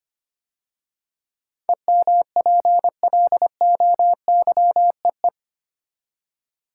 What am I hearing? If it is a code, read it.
Morse code, heard as a keyed tone, about 25 wpm, 700 Hz. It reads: EMPLOYEE
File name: 8 Letter Words - First 3 Letters Given - 25wpm